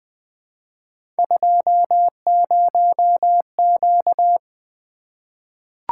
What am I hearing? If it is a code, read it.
Morse code, heard as a keyed tone, about 20 wpm, 700 Hz. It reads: 20Q
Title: Three Letter-Number Combinations - 20wpm